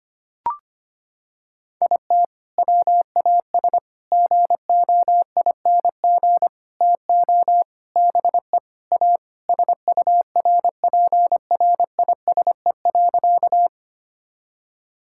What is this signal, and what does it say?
Morse code, heard as a keyed tone, about 25 wpm, 700 Hz. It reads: IT WAS GOING TO BE A SURPRISE.
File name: practice / 25wpm / Sentences with 4-Word Phrases - 25wpm